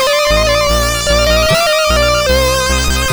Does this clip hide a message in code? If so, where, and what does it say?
No.